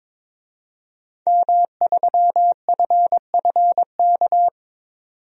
Morse code, heard as a keyed tone, about 22 wpm, 700 Hz. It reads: M3FFK
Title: DXCC-100 Call Signs - 5 Characters - 22wpm